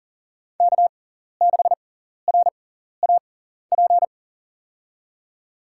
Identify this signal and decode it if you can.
Morse code, heard as a keyed tone, about 40 wpm, spaced as for 14 wpm, 700 Hz. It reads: K6RAP
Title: USA Call Signs - 5 Characters - 6x Character Spacing - 40wpm